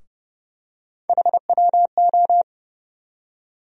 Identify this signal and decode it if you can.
Morse code, heard as a keyed tone, about 30 wpm, 700 Hz. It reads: HWO
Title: USA Call Sign Postfixes - 3 Letters 30wpm